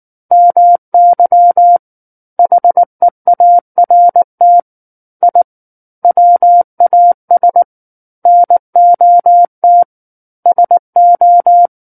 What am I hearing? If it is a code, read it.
Morse code, heard as a keyed tone, about 19 wpm, 700 Hz. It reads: MY HEART I WAS NOT SO
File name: Book One / 10-Chapter 10 - 19 WPM